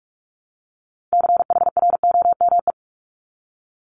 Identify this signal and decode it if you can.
Morse code, heard as a keyed tone, about 46 wpm, 690 Hz. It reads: CHROME
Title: Multi-letter Phonemes Repeated 3x 45wpm